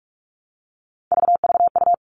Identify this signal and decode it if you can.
Morse code, heard as a keyed tone, about 44 wpm, 700 Hz. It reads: VVU
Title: All Cut Numbers - 3-Digit 45wpm